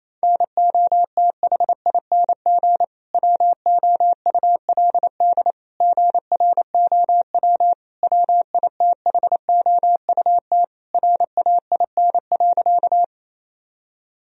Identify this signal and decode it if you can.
Morse code, heard as a keyed tone, about 28 wpm, 700 Hz. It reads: NOTHING WOULD GROW WITHOUT RAIN.
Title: Sentences from Top 1000 Words - Review Part 3 - 28wpm